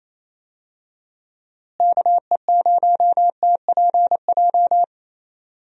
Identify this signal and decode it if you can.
Morse code, heard as a keyed tone, about 28 wpm, 700 Hz. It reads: KE0TPJ
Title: USA Call Signs - 6 Characters - Repeated 2x - 28wpm